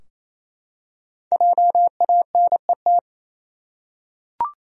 Morse code, heard as a keyed tone, about 27 wpm, 700 Hz. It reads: JANET
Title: USA Common Names - Max 5 Letters - 28wpm